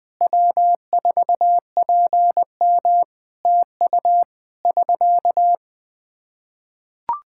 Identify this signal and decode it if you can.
Morse code, heard as a keyed tone, about 21 wpm, 700 Hz. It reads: W4PM TU <SK>
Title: Practice QSO - Exchange 4 - Ending - 20wpm